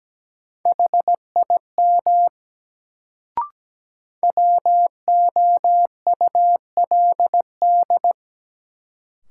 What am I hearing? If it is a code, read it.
Morse code, heard as a keyed tone, about 17 wpm, 700 Hz. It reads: HIM WOULD
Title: Sentences from Top 100 Words - New Word Drill - 17wpm